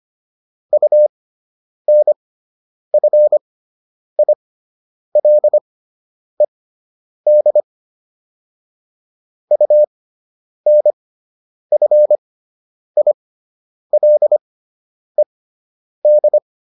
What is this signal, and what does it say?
Morse code, heard as a keyed tone, about 25 wpm, spaced as for 9 wpm, 600 Hz. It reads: UNFILED UNFILED